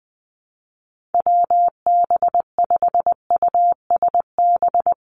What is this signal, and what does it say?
Morse code, heard as a keyed tone, about 20 wpm, 700 Hz. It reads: WB5USB
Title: NA SOTA Activator Callsigns 20wpm